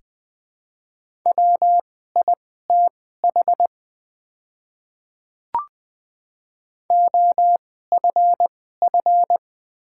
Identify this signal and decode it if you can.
Morse code, heard as a keyed tone, about 21 wpm, spaced as for 15 wpm, 700 Hz. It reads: WITH OFF
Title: 2x Character Spacing - Top 300 Words - 20wpm